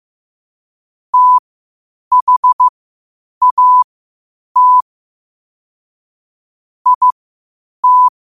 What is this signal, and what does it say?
Morse code, heard as a keyed tone, about 15 wpm, spaced as for 8 wpm, 1000 Hz. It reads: THAT IT